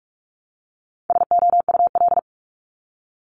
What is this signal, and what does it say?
Morse code, heard as a keyed tone, about 46 wpm, 700 Hz. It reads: SOUL
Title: Multi-letter Phonemes 45wpm